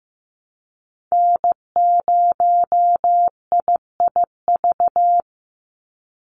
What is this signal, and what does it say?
Morse code, heard as a keyed tone, about 15 wpm, 700 Hz. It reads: N0IIV
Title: USA Call Signs 15wpm